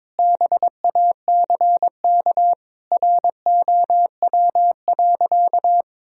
Morse code, heard as a keyed tone, about 22 wpm, 700 Hz. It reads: BACK ROW.
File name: practice / 22wpm / Sentences with 3-Word Phrases - 22wpm